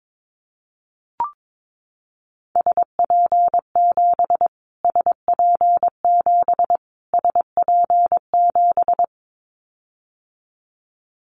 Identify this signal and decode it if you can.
Morse code, heard as a keyed tone, about 22 wpm, 700 Hz. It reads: SP7 SP7 SP7
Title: Callsign Prefixes for Top 25 DXCC - Repeated 3x - 22wpm